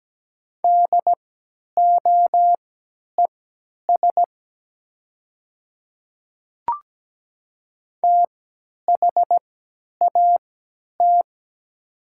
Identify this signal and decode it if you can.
Morse code, heard as a keyed tone, about 17 wpm, spaced as for 10 wpm, 700 Hz. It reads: DOES THAT
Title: 3x Character Spacing - Top 200 Words - 17wpm